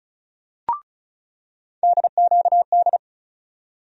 Morse code, heard as a keyed tone, about 36 wpm, 700 Hz. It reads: DQD
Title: USA Call Sign Postfixes - 3 Letters 35wpm